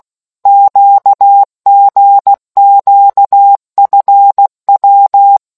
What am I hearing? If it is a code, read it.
Morse code, heard as a keyed tone, about 16 wpm, 800 Hz. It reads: QGQFW